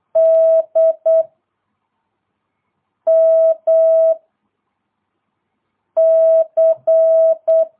Morse code, heard as a keyed tone, about 8 wpm, 660 Hz. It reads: D M C
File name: jh8cbh / 2022cw